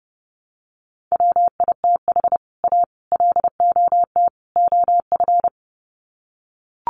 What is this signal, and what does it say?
Morse code, heard as a keyed tone, about 30 wpm, 700 Hz. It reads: WITH A LOT OF
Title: Sets of 4 Words 30wpm